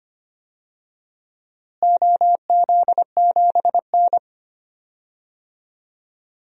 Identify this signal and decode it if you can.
Morse code, heard as a keyed tone, about 25 wpm, 700 Hz. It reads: OZ7N
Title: DXCC-100 Call Signs - 4 Characters - Repeated 3x - 25wpm